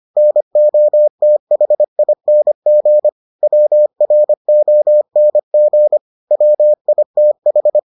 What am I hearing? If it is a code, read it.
Morse code, heard as a keyed tone, about 25 wpm, 600 Hz. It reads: NOTHING WRONG WITH